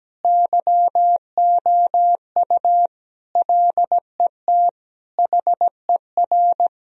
Morse code, heard as a keyed tone, about 17 wpm, 700 Hz. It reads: YOU LET HER
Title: Sentences from Top 1000 Words - Review Part 1 - 17wpm